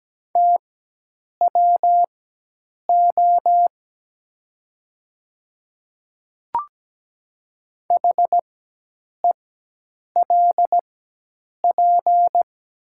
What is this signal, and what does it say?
Morse code, heard as a keyed tone, about 17 wpm, spaced as for 8 wpm, 700 Hz. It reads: TWO HELP
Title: 4x Character Spacing - Top 200 Words - 17wpm